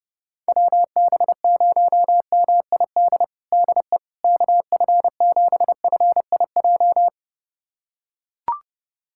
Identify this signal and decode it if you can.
Morse code, heard as a keyed tone, about 29 wpm, 700 Hz. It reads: WB0MID DE KF7FIJ
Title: USA Callsign DE Callsign - 6 Characters - 30wpm